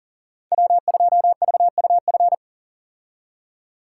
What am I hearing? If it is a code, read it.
Morse code, heard as a keyed tone, about 40 wpm, 700 Hz. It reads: W2VUF
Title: K1USN SST QSO Practice - 40wpm